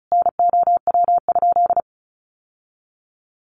Morse code, heard as a keyed tone, about 34 wpm, 700 Hz. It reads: NOW?
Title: Sentences from Top 600 Words 35wpm